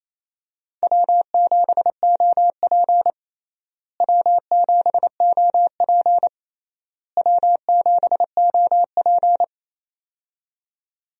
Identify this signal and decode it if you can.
Morse code, heard as a keyed tone, about 28 wpm, 700 Hz. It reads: W7OP W7OP W7OP
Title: USA Call Signs - 4 Characters - Repeated 3x - 28wpm